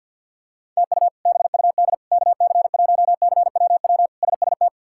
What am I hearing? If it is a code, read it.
Morse code, heard as a keyed tone, about 50 wpm, 700 Hz. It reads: TU BUD KC1CWW SST
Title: K1USN SST QSO Practice - 50wpm